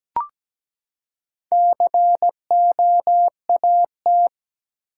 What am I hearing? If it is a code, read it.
Morse code, heard as a keyed tone, about 17 wpm, 700 Hz. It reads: COAT
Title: Four Letter Words - 17wpm